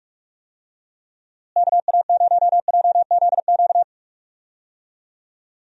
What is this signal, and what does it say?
Morse code, heard as a keyed tone, about 46 wpm, 700 Hz. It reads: KA0JZQ
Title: USA Call Signs - 6 Characters - Repeated 3x - 45wpm